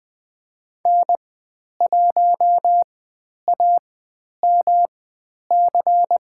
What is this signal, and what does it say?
Morse code, heard as a keyed tone, about 20 wpm, spaced as for 10 wpm, 700 Hz. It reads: N1AMC
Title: 109 Lesson 18g - Letter m - Call Signs 20wpm